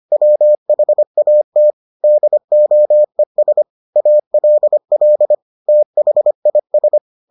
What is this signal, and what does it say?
Morse code, heard as a keyed tone, about 25 wpm, 600 Hz. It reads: WHAT DOES ALL THIS